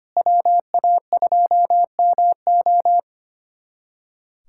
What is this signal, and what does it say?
Morse code, heard as a keyed tone, about 25 wpm, 700 Hz. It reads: WA2MO
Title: USA Callsign DE Callsign - 5 Characters - 25wpm